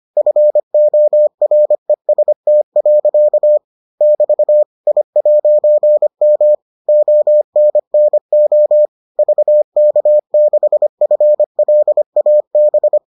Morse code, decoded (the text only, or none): FOREST. <BT> I'M ONNO VK6FLAB